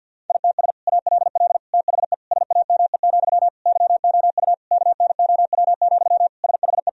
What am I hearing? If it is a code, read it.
Morse code, heard as a keyed tone, about 50 wpm, 700 Hz. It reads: ITS ALL THE SAME, YOU KNOW, SHE
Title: Sentences from Top 1000 Words - Review Part 3 - 50wpm